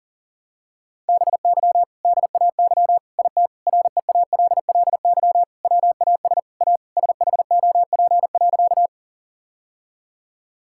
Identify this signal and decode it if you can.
Morse code, heard as a keyed tone, about 40 wpm, 700 Hz. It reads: BY DAY IT REALLY WAS A SHOP.
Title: Sentences from Top 1600 Words 40wpm